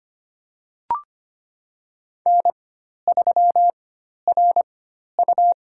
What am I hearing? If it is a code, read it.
Morse code, heard as a keyed tone, about 25 wpm, spaced as for 12 wpm, 690 Hz. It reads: N3RU
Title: USA Call Signs - 4 Characters - 4x Character Spacing - 25wpm